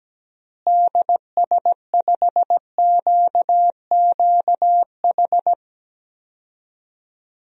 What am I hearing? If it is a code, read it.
Morse code, heard as a keyed tone, about 17 wpm, 700 Hz. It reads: DS5QQH
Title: DXCC-100 Call Signs - 6 Characters - Repeated 3x - 17wpm